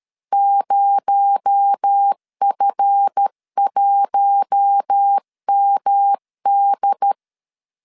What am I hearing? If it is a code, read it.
Morse code, heard as a keyed tone, about 13 wpm, 800 Hz. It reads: 0F1MD
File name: LEZIONI / 7 GF9Z / 7 GF9Z 12wpm 2147